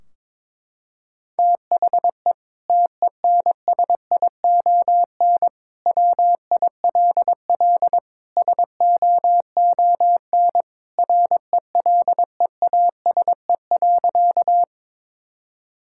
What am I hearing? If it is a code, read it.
Morse code, heard as a keyed tone, about 22 wpm, 700 Hz. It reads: THE TENSION WILL SOON RELEASE.
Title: Sentences from Top 1600 Words 22wpm